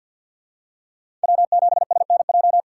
Morse code, heard as a keyed tone, about 46 wpm, 700 Hz. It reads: W7SNJ
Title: USA Call Signs - 5 Characters - 2x Character Spacing - 50wpm